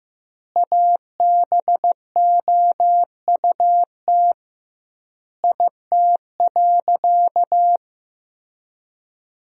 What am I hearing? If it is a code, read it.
Morse code, heard as a keyed tone, about 15 wpm, 700 Hz. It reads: ABOUT IT.
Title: 2x Word-Spacing - Sentences from Top 500 Words - 15wpm